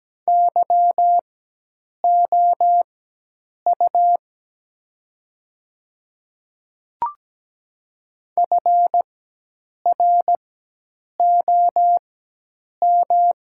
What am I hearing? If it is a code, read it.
Morse code, heard as a keyed tone, about 17 wpm, spaced as for 8 wpm, 700 Hz. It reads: YOU FROM
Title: 4x Character Spacing - Top 100 Words - 17wpm